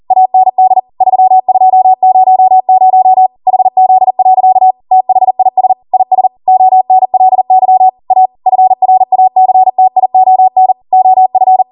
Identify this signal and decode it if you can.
Morse code, heard as a keyed tone, about 40 wpm, 750 Hz. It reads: AND 3200 HZ. THIS IS ONLY A FRACTION OF